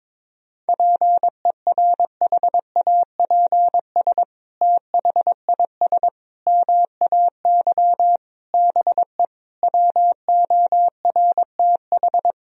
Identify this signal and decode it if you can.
Morse code, heard as a keyed tone, about 21 wpm, 700 Hz. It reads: PERHAPS THIS MAY BE WORTH
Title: Sentences from Top 1000 Words - Review Part 2 - 22wpm